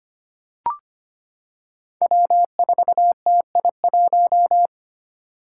Morse code, heard as a keyed tone, about 25 wpm, 700 Hz. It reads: W4TI1
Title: Five Challenging Letter-Number Combinations 25wpm